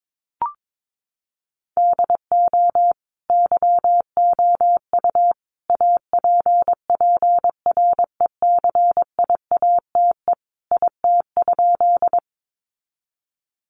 Morse code, heard as a keyed tone, about 22 wpm, 700 Hz. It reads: DO YOU APPRECIATE IT?
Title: Sentences from Top 1300 Words 22wpm